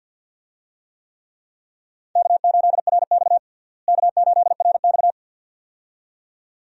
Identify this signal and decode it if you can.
Morse code, heard as a keyed tone, about 48 wpm, 700 Hz. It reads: K8RX K8RX
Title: USA Call Signs - 4 Characters - Repeated 2x - 50wpm